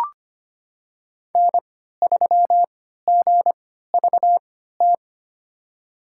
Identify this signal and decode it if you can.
Morse code, heard as a keyed tone, about 25 wpm, spaced as for 14 wpm, 700 Hz. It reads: N3GVT
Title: USA Call Signs - 5 Characters - 3x Character Spacing - 25wpm